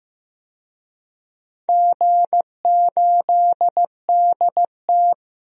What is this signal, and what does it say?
Morse code, heard as a keyed tone, about 15 wpm, 700 Hz. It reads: G8DT
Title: DXCC-100 Call Signs - 4 Characters - 15wpm